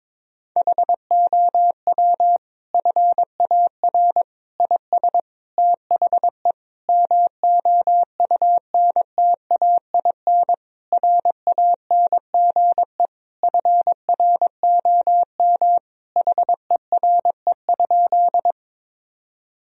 Morse code, heard as a keyed tone, about 22 wpm, 700 Hz. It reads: HOW FAR IS THE MOUNTAIN RANGE FROM HERE?
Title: Sentences from Top 1000 Words - Review Part 4 - 22wpm